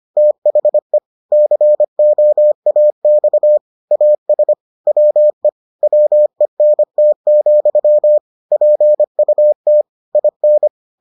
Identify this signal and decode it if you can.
Morse code, heard as a keyed tone, about 25 wpm, 600 Hz. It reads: THE COAX AS WE WENT, PUT IN